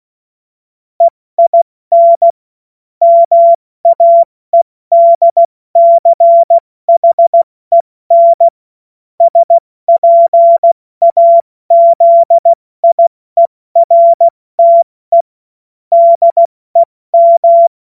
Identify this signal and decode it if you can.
Morse code, heard as a keyed tone, about 16 wpm, 680 Hz. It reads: EIN MAEDCHEN SPAZIERTE DEM
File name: Morsekurs / Witze / Witz 08. November 2022 70 F80